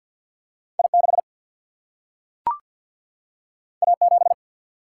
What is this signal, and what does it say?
Morse code, heard as a keyed tone, about 52 wpm, 700 Hz. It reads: I6 A7
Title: Single-Letter Single-Number 50wpm